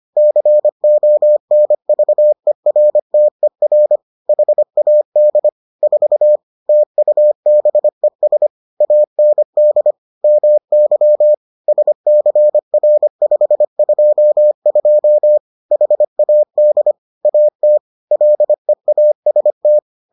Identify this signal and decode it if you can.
Morse code, heard as a keyed tone, about 25 wpm, 600 Hz. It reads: CONVERTER HAD 4 TUBES AND MY SCR522 HAD AT LEAST